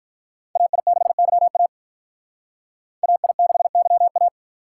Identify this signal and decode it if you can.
Morse code, heard as a keyed tone, about 45 wpm, 700 Hz. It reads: AI6YA AI6YA